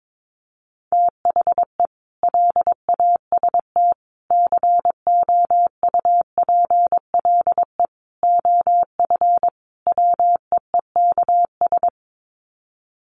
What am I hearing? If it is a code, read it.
Morse code, heard as a keyed tone, about 22 wpm, 700 Hz. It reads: THE LAST COUPLE OF WEEKS
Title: Sets of 5 Words Encore2 22wpm